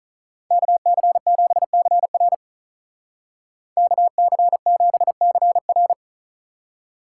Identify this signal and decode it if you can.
Morse code, heard as a keyed tone, about 44 wpm, 700 Hz. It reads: KC7CR KC7CR
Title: Speed Racing - USA Call Signs 35wpm